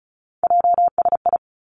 Morse code, heard as a keyed tone, about 36 wpm, 700 Hz. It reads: JSI
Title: Three Challenging Letter-Number Combinations 35wpm